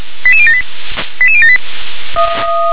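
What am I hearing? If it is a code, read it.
Five-tone selective call (ZVEI), heard as quick runs of five tones, tones of about 70 ms. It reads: 88078 88078
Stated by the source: ZVEI five-tone selective call received on SDR.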